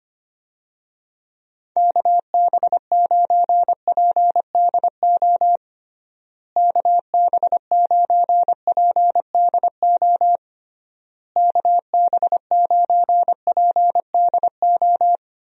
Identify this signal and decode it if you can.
Morse code, heard as a keyed tone, about 25 wpm, 700 Hz. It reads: KB9PDO KB9PDO KB9PDO
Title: USA Call Signs - 6 Characters - Repeated 3x - 25wpm